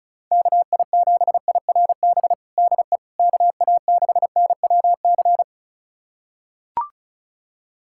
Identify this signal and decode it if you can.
Morse code, heard as a keyed tone, about 36 wpm, 700 Hz. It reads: KI7IRB DE KA6NWC
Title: USA Callsign DE Callsign - 6 Characters - 35wpm